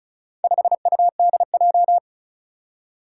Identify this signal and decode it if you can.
Morse code, heard as a keyed tone, about 36 wpm, 700 Hz. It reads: 5UDJ
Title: Four Challenging Letter-Number Combinations 35wpm